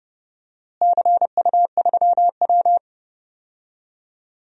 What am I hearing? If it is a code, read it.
Morse code, heard as a keyed tone, about 28 wpm, 700 Hz. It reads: CU3W
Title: DXCC-100 Call Signs - 4 Characters - Repeated 3x - 30wpm